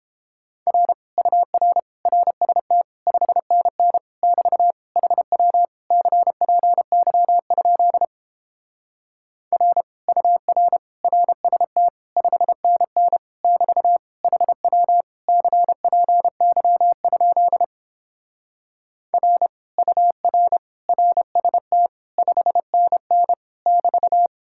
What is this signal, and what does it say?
Morse code, heard as a keyed tone, about 29 wpm, 690 Hz. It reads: R UR RST 5NN <BT> HW CPY? R UR RST 5NN <BT> HW CPY? R UR RST 5NN <BT>